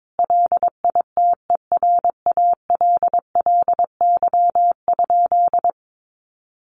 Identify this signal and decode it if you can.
Morse code, heard as a keyed tone, about 22 wpm, 700 Hz. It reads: LITERALLY?
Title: Sentences from Top 1400 Words 22wpm